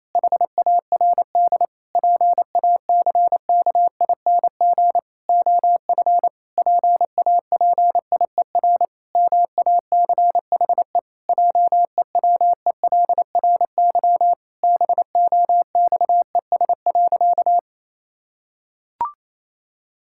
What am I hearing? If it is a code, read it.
Morse code, heard as a keyed tone, about 28 wpm, 700 Hz. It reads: HARD PACKING OF PAPIER MACHE JEWELRY BOXES.
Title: Pangram Challenge v2 - 28wpm